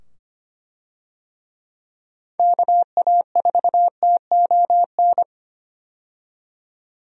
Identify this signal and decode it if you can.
Morse code, heard as a keyed tone, about 25 wpm, 700 Hz. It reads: KA4TON
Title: USA Call Signs - 6 Characters - 25wpm